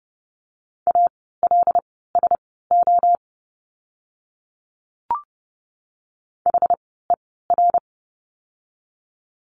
Morse code, heard as a keyed tone, about 30 wpm, spaced as for 17 wpm, 700 Hz. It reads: ALSO HER
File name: course / 30:10wpm / 115 Lesson 19f - Letter w - Words 30wpm